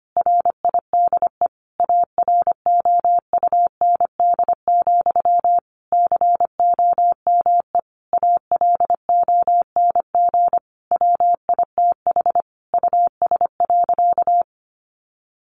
Morse code, heard as a keyed tone, about 25 wpm, 700 Hz. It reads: RIDE AROUND, COME ALONG WITH US.
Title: Sentences from Top 1200 Words 25wpm